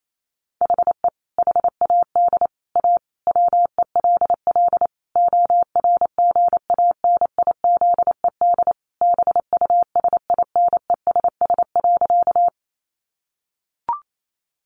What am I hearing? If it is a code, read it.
Morse code, heard as a keyed tone, about 28 wpm, 700 Hz. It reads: HE HAD A WELL ORGANIZED BUSINESS.
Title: Sentences from Top 1300 Words 28wpm